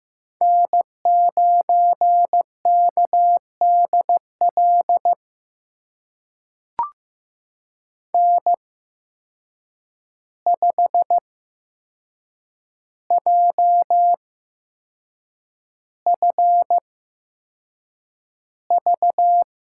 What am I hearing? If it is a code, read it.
Morse code, heard as a keyed tone, about 15 wpm, spaced as for 6 wpm, 700 Hz. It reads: N9KDL N5JFV
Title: USA Call Signs - 5 Characters - 8x Character Spacing - 15wpm